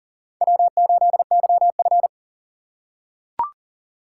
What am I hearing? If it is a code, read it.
Morse code, heard as a keyed tone, about 40 wpm, 700 Hz. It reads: W8YF
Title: USA Call Signs - 4 Characters - 3x Character Spacing - 40wpm